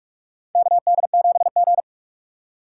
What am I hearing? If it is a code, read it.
Morse code, heard as a keyed tone, about 46 wpm, 700 Hz. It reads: KD7G
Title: USA Call Signs - 4 Characters - 2x Character Spacing - 45wpm